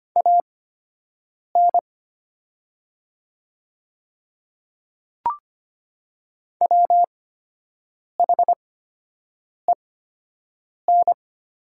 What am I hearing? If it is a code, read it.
Morse code, heard as a keyed tone, about 25 wpm, spaced as for 7 wpm, 700 Hz. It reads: AN WHEN
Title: 8x Character Spacing - Top 300 Words - 25wpm